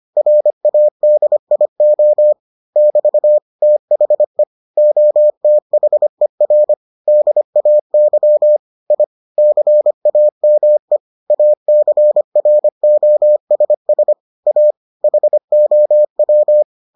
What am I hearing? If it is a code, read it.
Morse code, heard as a keyed tone, about 25 wpm, 600 Hz. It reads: RADIO <BT> THE OTHER DAY I CAME ACROSS A HOW